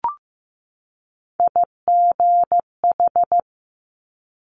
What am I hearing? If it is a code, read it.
Morse code, heard as a keyed tone, about 15 wpm, 690 Hz. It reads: IGH
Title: Multi-letter Phonemes 15wpm